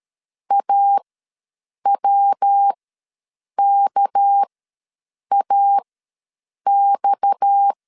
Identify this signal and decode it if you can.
Morse code, heard as a keyed tone, about 13 wpm, spaced as for 7 wpm, 800 Hz. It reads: AWKAX
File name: LEZIONI / 9 286 ws 12wpm 3351